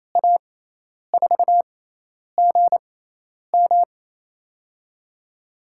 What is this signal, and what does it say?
Morse code, heard as a keyed tone, about 28 wpm, 700 Hz. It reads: A 4 G M
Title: USA Call Signs - 6x Character Spacing - 28wpm